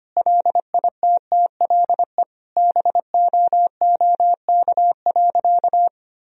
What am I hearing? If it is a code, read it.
Morse code, heard as a keyed tone, about 25 wpm, 700 Hz. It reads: LITTLE BOOK.
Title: Sentences from Top 1000 Words - Review Part 2 - 25wpm